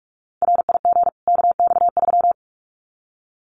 Morse code, heard as a keyed tone, about 48 wpm, 700 Hz. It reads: RIG KX3
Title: Common QSO Phrases 45wpm